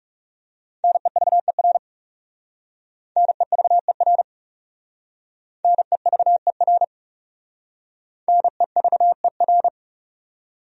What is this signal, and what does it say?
Morse code, heard as a keyed tone, about 35 wpm, 700 Hz. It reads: NEVER NEVER NEVER NEVER
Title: Speed-Racing Top 500 Words 30wpm